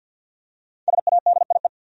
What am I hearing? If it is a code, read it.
Morse code, heard as a keyed tone, about 47 wpm, 700 Hz. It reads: SADIE
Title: USA Common Names - 50wpm